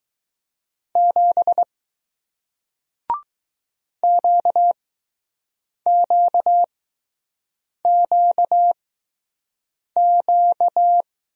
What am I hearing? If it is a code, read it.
Morse code, heard as a keyed tone, about 20 wpm, spaced as for 6 wpm, 700 Hz. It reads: 7 QQQQ